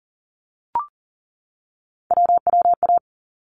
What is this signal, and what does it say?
Morse code, heard as a keyed tone, about 40 wpm, 700 Hz. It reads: WWA
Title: ARRL Sections 40wpm